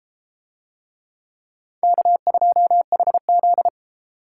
Morse code, heard as a keyed tone, about 33 wpm, 700 Hz. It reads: K2HZ